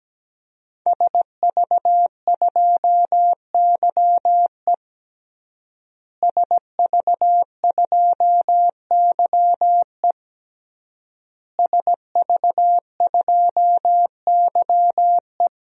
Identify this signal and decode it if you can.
Morse code, heard as a keyed tone, about 17 wpm, 700 Hz. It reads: SV2YE SV2YE SV2YE
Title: DXCC-100 Call Signs - 5 Characters - Repeated 3x - 17wpm